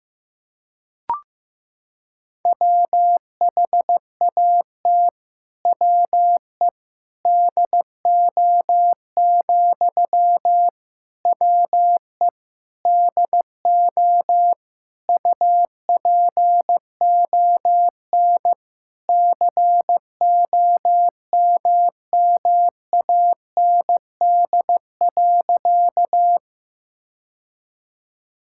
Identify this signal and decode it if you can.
Morse code, heard as a keyed tone, about 16 wpm, 700 Hz. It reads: WHAT WE DO, WE DO UPON COMMAND.
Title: Sentences from Top 1600 Words 15wpm